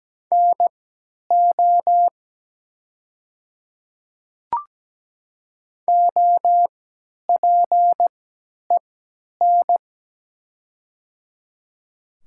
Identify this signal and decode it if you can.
Morse code, heard as a keyed tone, about 17 wpm, spaced as for 10 wpm, 700 Hz. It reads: NO OPEN
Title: 3x Character Spacing - Top 300 Words - 17wpm